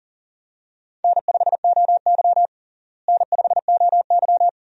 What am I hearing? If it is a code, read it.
Morse code, heard as a keyed tone, about 40 wpm, 700 Hz. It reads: N5OY N5OY
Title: USA Call Signs - 4 Characters - Repeated 2x - 40wpm